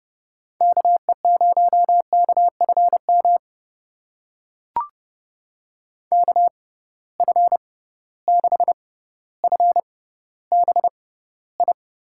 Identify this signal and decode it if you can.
Morse code, heard as a keyed tone, about 30 wpm, spaced as for 15 wpm, 700 Hz. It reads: KE0KFM KF6FBI
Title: USA Call Signs - 6 Characters - 6x Character Spacing - 30wpm